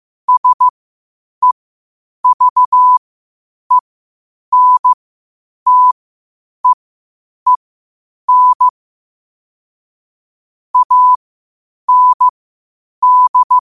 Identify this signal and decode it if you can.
Morse code, heard as a keyed tone, about 15 wpm, spaced as for 8 wpm, 1000 Hz. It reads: SEVENTEEN AND
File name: qotdfiles / 8wpm